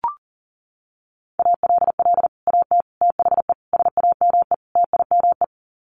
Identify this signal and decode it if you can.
Morse code, heard as a keyed tone, about 40 wpm, 700 Hz. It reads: ALL AT THE SAME TIME
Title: Sets of 5 Words Encore2 40wpm